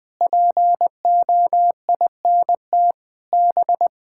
Morse code, heard as a keyed tone, about 20 wpm, 700 Hz. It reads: POINT B
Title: Sets of 6 Words - 20wpm